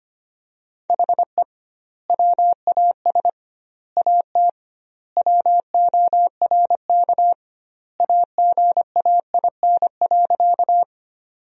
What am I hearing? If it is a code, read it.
Morse code, heard as a keyed tone, about 25 wpm, 700 Hz. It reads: HE WAS AT WORK AGAIN.
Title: 2x Word-Spacing - Sentences from Top 200 Words - 25wpm